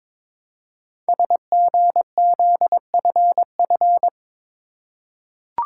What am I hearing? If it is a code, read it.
Morse code, heard as a keyed tone, about 22 wpm, 700 Hz. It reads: SGZFF
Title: Code Groups - 5 Letters - 22wpm